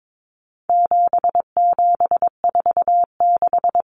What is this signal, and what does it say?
Morse code, heard as a keyed tone, about 22 wpm, 700 Hz. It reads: 7746
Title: Numbers - 4-Digit 22wpm